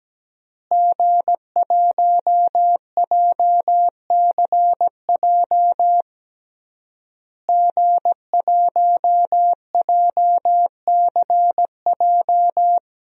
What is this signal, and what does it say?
Morse code, heard as a keyed tone, about 17 wpm, 700 Hz. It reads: G1JCJ G1JCJ
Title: DXCC-100 Call Signs - 5 Characters - Repeated 3x - 17wpm